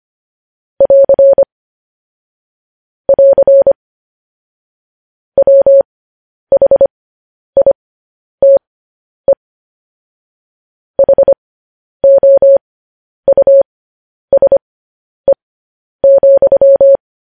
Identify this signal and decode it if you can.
Morse code, heard as a keyed tone, about 25 wpm, spaced as for 10 wpm, 550 Hz. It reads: <AR> <AR> WHITE HOUSE,
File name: podcast / News headlines-1x-25x10